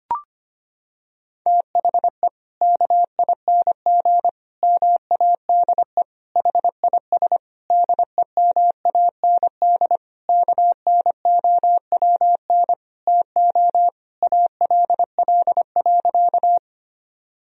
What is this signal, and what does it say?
Morse code, heard as a keyed tone, about 25 wpm, 700 Hz. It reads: THE KING MADE HIS DEMAND KNOWN TO ALL.
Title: Sentences from Top 1000 Words - Review Part 4 - 25wpm